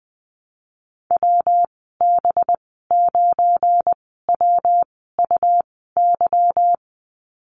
Morse code, heard as a keyed tone, about 20 wpm, spaced as for 14 wpm, 700 Hz. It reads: WB9WUY
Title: 219 Lesson 35g - Number 0 - Call Signs 20wpm